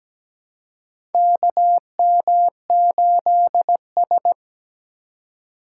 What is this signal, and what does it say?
Morse code, heard as a keyed tone, about 17 wpm, 700 Hz. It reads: KM8S
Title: USA Call Signs - 2x Character Spacing - 17wpm